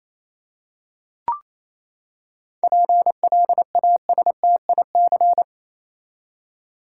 Morse code, heard as a keyed tone, about 28 wpm, 700 Hz. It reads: PLASTIC
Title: Seven Letter Words - 28wpm